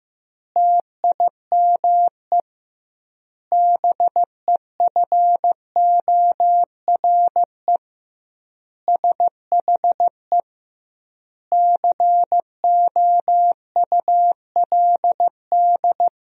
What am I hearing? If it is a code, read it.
Morse code, heard as a keyed tone, about 16 wpm, 700 Hz. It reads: TIME BEFORE SHE COULD
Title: 2x Word-Spacing - Sentences from Top 500 Words - 15wpm